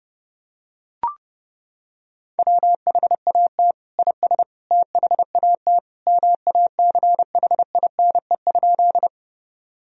Morse code, heard as a keyed tone, about 30 wpm, 700 Hz. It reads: WHAT IS THAT MACHINE?
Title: Sentences from Top 1000 Words - Review Part 2 - 30wpm